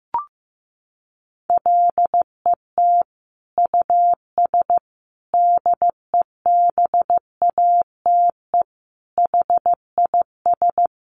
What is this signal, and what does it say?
Morse code, heard as a keyed tone, about 15 wpm, 700 Hz. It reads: LET US DEBATE HIS
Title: Sentences from Top 1100 Words 15wpm